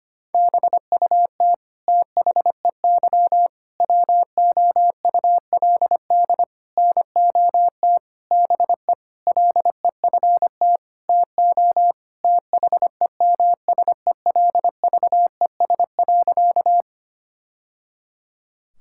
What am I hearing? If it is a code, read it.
Morse code, heard as a keyed tone, about 25 wpm, 700 Hz. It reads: BUT THEY WOULD NOT BE LEFT TO THEMSELVES.
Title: Sentences from Top 1000 Words - Review Part 5 - 25wpm